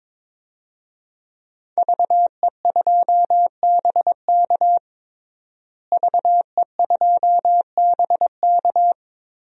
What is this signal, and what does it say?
Morse code, heard as a keyed tone, about 22 wpm, 700 Hz. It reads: VE2BK VE2BK
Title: DXCC-100 Call Signs - 5 Characters - Repeated 3x - 22wpm